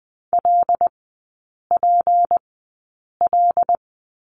Morse code, heard as a keyed tone, about 20 wpm, spaced as for 8 wpm, 690 Hz. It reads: LPL
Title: 159 Lesson 26e - Letter g - Three Characters 20wpm